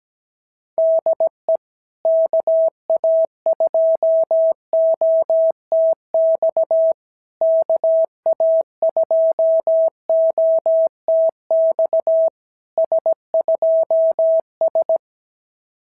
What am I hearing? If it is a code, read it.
Morse code, heard as a keyed tone, about 17 wpm, 650 Hz. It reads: DE KA2OTX KA2OTX S2S
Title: SOTA QSO Practice 17wpm